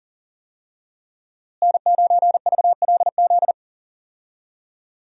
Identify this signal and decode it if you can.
Morse code, heard as a keyed tone, about 40 wpm, 700 Hz. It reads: N9VLZ
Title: USA Call Signs - 5 Characters - Repeated 2x - 40wpm